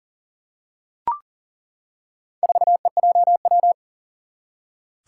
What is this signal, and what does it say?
Morse code, heard as a keyed tone, about 38 wpm, 700 Hz. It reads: 4EJW
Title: Four Challenging Letter-Number Combinations 40wpm